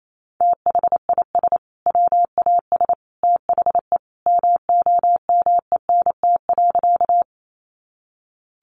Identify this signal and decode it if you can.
Morse code, heard as a keyed tone, about 28 wpm, 700 Hz. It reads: THIS WAS THE MOMENT.
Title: Sentences from Top 1000 Words 28wpm